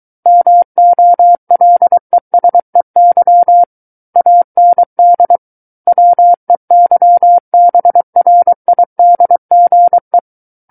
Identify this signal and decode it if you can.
Morse code, heard as a keyed tone, about 23 wpm, 700 Hz. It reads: MOLESEY AND WEYBRIDGE
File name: Book One / 14-Chapter 14 - 23 WPM